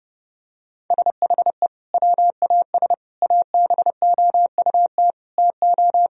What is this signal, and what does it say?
Morse code, heard as a keyed tone, about 30 wpm, 700 Hz. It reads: SHE WAS ABOUT TO
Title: Sentences from Top 1000 Words - Review Part 2 - 30wpm